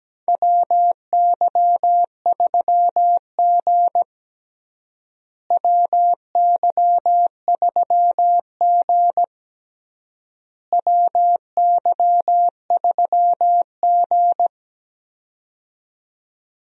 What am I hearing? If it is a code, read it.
Morse code, heard as a keyed tone, about 17 wpm, 700 Hz. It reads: WY3G WY3G WY3G
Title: USA Call Signs - 4 Characters - Repeated 3x - 17wpm